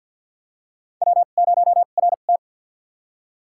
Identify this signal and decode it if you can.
Morse code, heard as a keyed tone, about 51 wpm, spaced as for 36 wpm, 700 Hz. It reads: W0RT
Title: USA Call Signs - 4 Characters - 2x Character Spacing - 50wpm